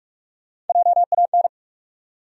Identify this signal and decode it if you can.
Morse code, heard as a keyed tone, about 47 wpm, 700 Hz. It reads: JAN
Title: USA Common Names - Max 4 Letters - 45wpm